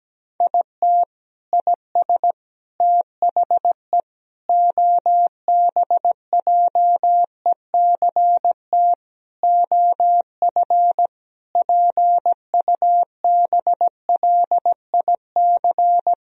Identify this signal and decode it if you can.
Morse code, heard as a keyed tone, about 17 wpm, 700 Hz. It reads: IT IS THE OBJECT OF PUBLIC
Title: Sentences from Top 1000 Words - Review Part 4 - 17wpm